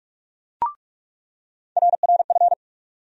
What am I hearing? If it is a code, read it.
Morse code, heard as a keyed tone, about 43 wpm, 700 Hz. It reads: RRF
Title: USA Call Sign Postfixes - 3 Letters 45wpm